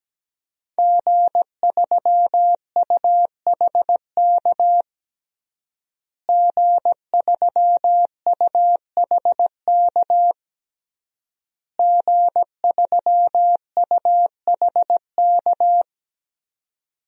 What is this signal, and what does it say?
Morse code, heard as a keyed tone, about 17 wpm, 700 Hz. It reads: G3UHK G3UHK G3UHK
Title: DXCC-100 Call Signs - 5 Characters - Repeated 3x - 17wpm